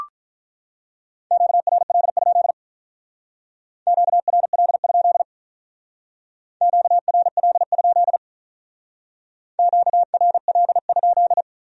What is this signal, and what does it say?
Morse code, heard as a keyed tone, about 42 wpm, 700 Hz. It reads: QRL? QRL? QRL? QRL?
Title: Speed-Racing Q-Codes - 35wpm